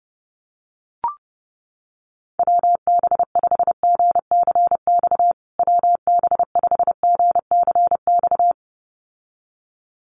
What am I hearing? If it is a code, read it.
Morse code, heard as a keyed tone, about 30 wpm, 700 Hz. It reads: WB5GCX WB5GCX
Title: USA Call Signs Repeated 2x 30wpm